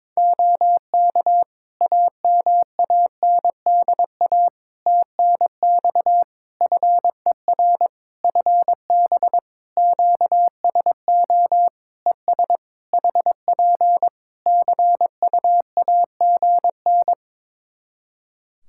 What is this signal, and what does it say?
Morse code, heard as a keyed tone, about 22 wpm, 700 Hz. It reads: OK AMANDA TNX FER FB QSO ES HP CUAGN
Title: K1USN SST QSO Practice 22wpm